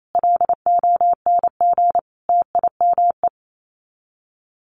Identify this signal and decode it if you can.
Morse code, heard as a keyed tone, about 28 wpm, 700 Hz. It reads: LONG TIME
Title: Sets of 5 Words 28wpm